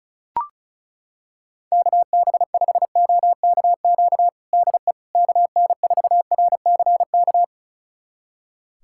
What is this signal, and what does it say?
Morse code, heard as a keyed tone, about 36 wpm, 700 Hz. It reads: KB5OKQ DE KN4RCK
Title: USA Callsign DE Callsign 35wpm